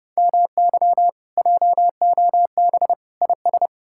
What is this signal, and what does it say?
Morse code, heard as a keyed tone, about 37 wpm, 700 Hz. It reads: MY JOB IS
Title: Speed-Racing Common QSO Phrases - 30wpm